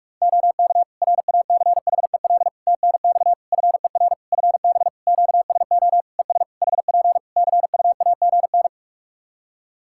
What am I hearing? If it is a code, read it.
Morse code, heard as a keyed tone, about 46 wpm, 700 Hz. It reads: OK RACHEL TNX FER FB QSO ES HP CUAGN
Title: K1USN SST QSO Practice 45wpm